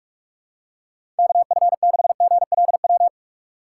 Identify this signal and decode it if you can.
Morse code, heard as a keyed tone, about 46 wpm, 700 Hz. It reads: KF6GLW
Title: USA Call Signs - 6 Characters - 4x Character Spacing - 45wpm